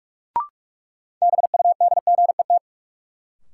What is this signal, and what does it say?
Morse code, heard as a keyed tone, about 41 wpm, 700 Hz. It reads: BUDGET
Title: Sentences from Top 1200 Words - New Word Drill - 45wpm